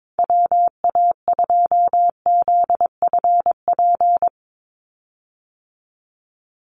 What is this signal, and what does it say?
Morse code, heard as a keyed tone, about 22 wpm, 700 Hz. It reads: WA2ZFP
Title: USA Call Signs 22wpm